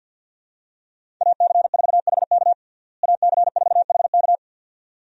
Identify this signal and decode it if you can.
Morse code, heard as a keyed tone, about 49 wpm, 700 Hz. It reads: AC4HK AC4HK
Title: USA Call Signs - 5 Characters - Repeated 2x - 50wpm